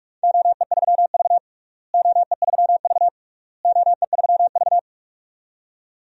Morse code, heard as a keyed tone, about 46 wpm, 700 Hz. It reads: OE3V OE3V OE3V
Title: DXCC-100 Call Signs - 4 Characters - Repeated 3x - 45wpm